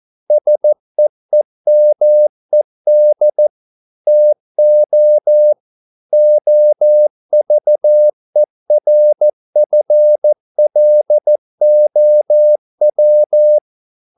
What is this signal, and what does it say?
Morse code, heard as a keyed tone, about 14 wpm, 600 Hz. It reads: SEEMED TO OVERFLOW